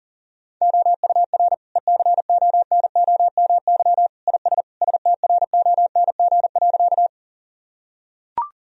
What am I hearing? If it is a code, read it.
Morse code, heard as a keyed tone, about 40 wpm, 700 Hz. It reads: OUR ECONOMY IS STRONG.
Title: Sentences from Top 1000 Words - Review Part 3 - 40wpm